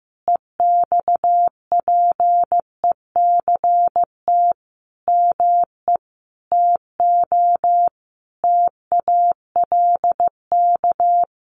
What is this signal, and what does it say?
Morse code, heard as a keyed tone, about 15 wpm, 700 Hz. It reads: EXPECT ME TO TALK
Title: Sentences from Top 1100 Words 15wpm